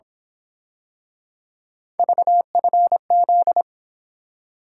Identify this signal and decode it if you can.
Morse code, heard as a keyed tone, about 26 wpm, 700 Hz. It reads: VFZ
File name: Speed-Racing - Three Letter-Number Combinations - 22wpm